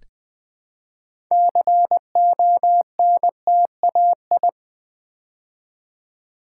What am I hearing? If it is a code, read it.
Morse code, heard as a keyed tone, about 20 wpm, 700 Hz. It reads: CONTAI
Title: Build A Word - 3 Letters At A Time - 9-Letter Words - 20wpm